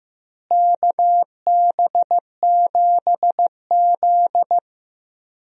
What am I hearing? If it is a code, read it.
Morse code, heard as a keyed tone, about 15 wpm, 700 Hz. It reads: KB7Z
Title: USA Call Signs - 4 Characters - Repeated 2x - 15wpm